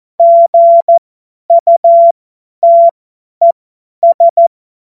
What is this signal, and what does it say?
Morse code, heard as a keyed tone, about 14 wpm, spaced as for 10 wpm, 680 Hz. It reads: GUTES
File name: Morsekurs / Witze / Witz 20. Dezember 2022 50 F70